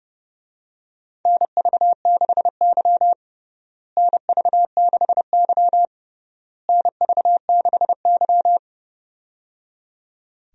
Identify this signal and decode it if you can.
Morse code, heard as a keyed tone, about 29 wpm, 700 Hz. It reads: NV6Y NV6Y NV6Y
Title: USA Call Signs - 4 Characters - Repeated 3x - 30wpm